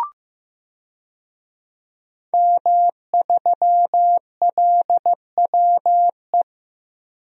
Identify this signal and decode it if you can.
Morse code, heard as a keyed tone, about 15 wpm, 700 Hz. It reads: M3LWE
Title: DXCC-100 Call Signs - 5 Characters - Repeated 2x - 15wpm